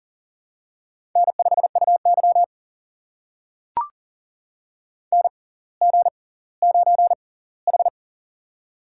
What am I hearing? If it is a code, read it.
Morse code, heard as a keyed tone, about 40 wpm, spaced as for 21 wpm, 700 Hz. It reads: N5UY NG9H
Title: USA Call Signs - 4 Characters - 6x Character Spacing - 40wpm